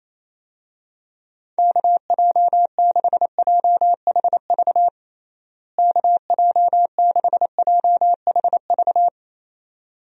Morse code, heard as a keyed tone, about 28 wpm, 700 Hz. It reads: KJ6JHV KJ6JHV
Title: USA Call Signs - 6 Characters - Repeated 2x - 28wpm